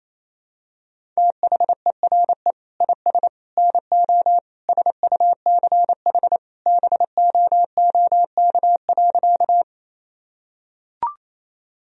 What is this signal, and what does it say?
Morse code, heard as a keyed tone, about 28 wpm, 700 Hz. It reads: THERE IS NO SUCH BOOK.
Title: Sentences from Top 1000 Words - Review Part 1 - 28wpm